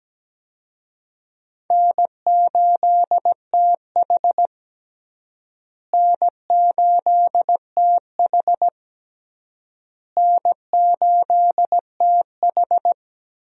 Morse code, heard as a keyed tone, about 17 wpm, 700 Hz. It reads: N8TH N8TH N8TH
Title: USA Call Signs - 4 Characters - Repeated 3x - 17wpm